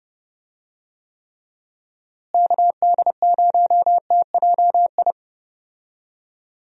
Morse code, heard as a keyed tone, about 30 wpm, 700 Hz. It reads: KD0TJI
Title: USA Call Signs - 6 Characters - Repeated 3x - 30wpm